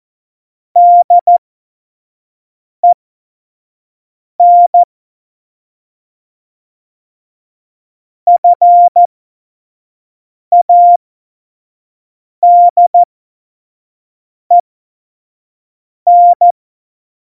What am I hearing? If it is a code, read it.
Morse code, heard as a keyed tone, about 14 wpm, spaced as for 5 wpm, 700 Hz. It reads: DEN FADEN
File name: Morsekurs / Witze / Witz 02. Oktober 2018, 25 F70